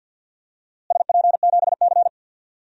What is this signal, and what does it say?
Morse code, heard as a keyed tone, about 52 wpm, 700 Hz. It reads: SP7C
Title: DXCC-100 Call Signs - 4 Characters - Repeated 3x - 50wpm